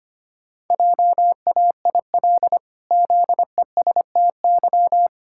Morse code, heard as a keyed tone, about 25 wpm, 700 Hz. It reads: JAIL ZESTY
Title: Pangram Challenge v2 - 25wpm